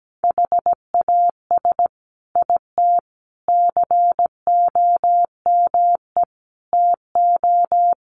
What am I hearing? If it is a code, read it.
Morse code, heard as a keyed tone, about 17 wpm, 700 Hz. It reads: HAS IT COME TO
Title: Sentences from Top 100 Words 17wpm